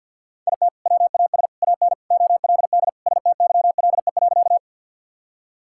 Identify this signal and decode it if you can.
Morse code, heard as a keyed tone, about 50 wpm, 700 Hz. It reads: IT WAS AN OLD STYLE.
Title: Sentences from Top 1000 Words - Review Part 5 - 50wpm